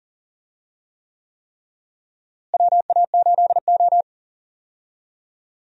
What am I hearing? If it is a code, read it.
Morse code, heard as a keyed tone, about 40 wpm, 700 Hz. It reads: WA8O
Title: USA Call Signs - 4 Characters - Repeated 3x - 40wpm